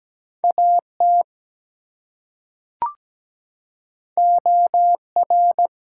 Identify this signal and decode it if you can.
Morse code, heard as a keyed tone, about 17 wpm, 700 Hz. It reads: AT OR
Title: Two Letter Words - 17wpm